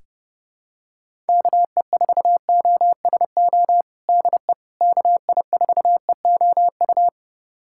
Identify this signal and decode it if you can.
Morse code, heard as a keyed tone, about 30 wpm, 700 Hz. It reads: KE4OSO DE KI4EOU
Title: USA Callsign DE Callsign - 6 Characters - 30wpm